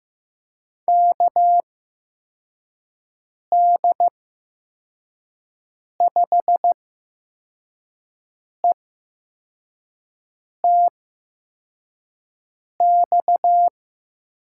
Morse code, heard as a keyed tone, about 15 wpm, spaced as for 4 wpm, 700 Hz. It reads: KD5ETX
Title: USA Call Signs - 8x Character Spacing - 15wpm